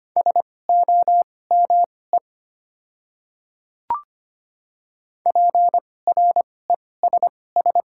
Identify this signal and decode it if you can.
Morse code, heard as a keyed tone, about 25 wpm, spaced as for 18 wpm, 690 Hz. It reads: SOME PRESS
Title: 2x Character Spacing - Top 300 Words - 25wpm